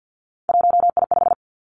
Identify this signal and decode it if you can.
Morse code, heard as a keyed tone, about 50 wpm, 700 Hz. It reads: 1I5
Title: Three Challenging Letter-Number Combinations 50wpm